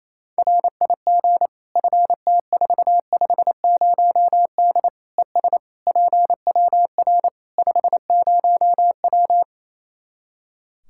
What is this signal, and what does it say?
Morse code, heard as a keyed tone, about 28 wpm, 700 Hz. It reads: RIG FT450D ES PWR 50W
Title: K1USN SST QSO Practice 28wpm